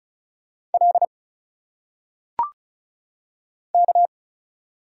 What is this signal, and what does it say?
Morse code, heard as a keyed tone, about 36 wpm, 700 Hz. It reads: L K
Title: Single Letters - Phonetic Pronunciation 35wpm